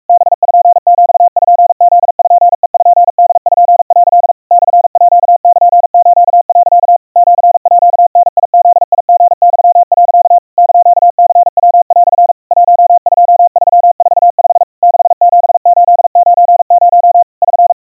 Morse code (Japanese Wabun code), heard as a keyed tone, about 44 wpm, 700 Hz. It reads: パピプペポ」 モールス、 シーダブリユ、 <KN>ニヤ) 12345 67890 <SN>
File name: 150 cpm 700 Hz Crispy